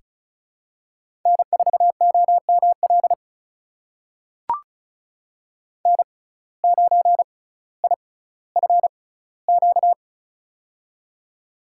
Morse code, heard as a keyed tone, about 36 wpm, spaced as for 18 wpm, 700 Hz. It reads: N4OML N9IFQ
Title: USA Call Signs - 5 Characters - 6x Character Spacing - 35wpm